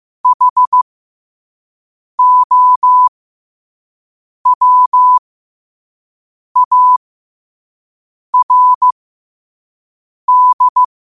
Morse code, heard as a keyed tone, about 15 wpm, spaced as for 5 wpm, 1000 Hz. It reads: HOWARD
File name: qotdfiles / 5wpm